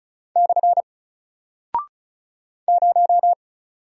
Morse code, heard as a keyed tone, about 36 wpm, 700 Hz. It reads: / 0
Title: Single Character (Letter:Number:Punctuation:Prosign) Alt 35wpm